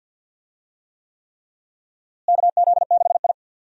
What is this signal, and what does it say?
Morse code, heard as a keyed tone, about 52 wpm, 700 Hz. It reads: KZ6I
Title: USA Call Signs - 4 Characters - Repeated 3x - 50wpm